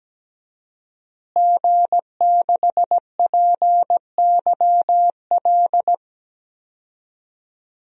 Morse code, heard as a keyed tone, about 17 wpm, 700 Hz. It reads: G6PYL
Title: DXCC-100 Call Signs - 5 Characters - Repeated 2x - 17wpm